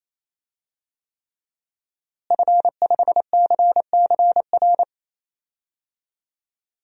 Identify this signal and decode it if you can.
Morse code, heard as a keyed tone, about 28 wpm, 700 Hz. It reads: F5CCR
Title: DXCC-100 Call Signs - 5 Characters - Repeated 3x - 28wpm